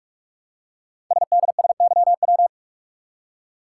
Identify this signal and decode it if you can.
Morse code, heard as a keyed tone, about 44 wpm, 700 Hz. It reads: SDSYW
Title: Code Groups - 5 Letter:Numbers - 45wpm